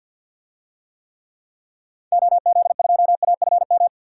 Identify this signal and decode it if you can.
Morse code, heard as a keyed tone, about 53 wpm, 700 Hz. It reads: OZ2AFM
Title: DXCC-100 Call Signs - 6 Characters - Repeated 2x - 50wpm